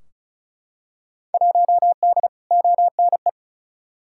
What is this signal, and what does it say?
Morse code, heard as a keyed tone, about 36 wpm, 700 Hz. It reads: 1D ONE
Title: Field Day Exchanges - 35wpm